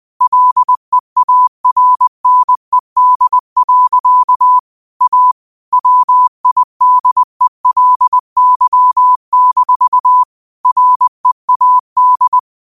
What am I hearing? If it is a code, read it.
Morse code, heard as a keyed tone, about 20 wpm, 1000 Hz. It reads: LEARNED. A WIDELY-READ